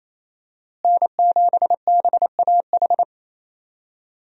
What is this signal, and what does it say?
Morse code, heard as a keyed tone, about 27 wpm, 700 Hz. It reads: N7BAH
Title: USA Call Signs - 5 Characters - Repeated 2x - 28wpm